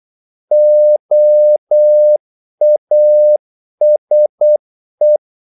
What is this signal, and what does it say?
Morse code, heard as a keyed tone, about 8 wpm, 600 Hz. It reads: OASE
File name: Korte ord DK 8wpm